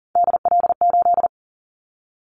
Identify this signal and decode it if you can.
Morse code, heard as a keyed tone, about 37 wpm, 700 Hz. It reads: DL8
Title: Callsign Prefixes for Top 25 DXCC - Repeated 3x - 40wpm